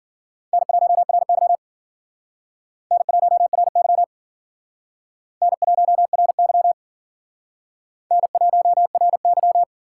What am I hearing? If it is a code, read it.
Morse code, heard as a keyed tone, about 50 wpm, 700 Hz. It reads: N1RY N1RY N1RY N1RY
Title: Speed Racing - USA 1x2 Call Signs 40wpm